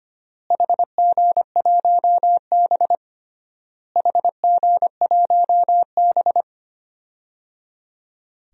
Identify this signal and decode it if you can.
Morse code, heard as a keyed tone, about 25 wpm, 700 Hz. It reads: HG1B HG1B
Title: DXCC-100 Call Signs - 4 Characters - Repeated 3x - 25wpm